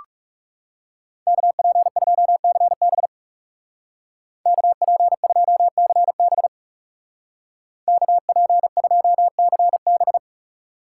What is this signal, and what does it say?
Morse code, heard as a keyed tone, about 36 wpm, 700 Hz. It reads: KP2CB KP2CB KP2CB
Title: Speed Racing - USA 2x2 Call Signs 30wpm